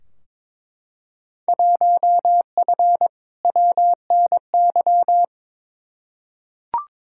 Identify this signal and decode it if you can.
Morse code, heard as a keyed tone, about 22 wpm, 700 Hz. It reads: 1F WNY
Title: Field Day Exchanges - 22wpm